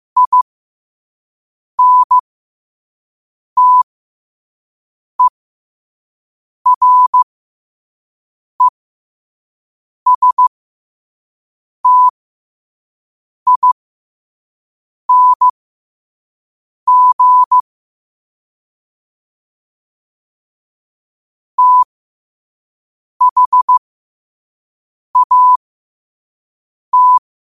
Morse code, heard as a keyed tone, about 15 wpm, spaced as for 5 wpm, 1000 Hz. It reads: INTERESTING THAT
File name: qotdfiles / 5wpm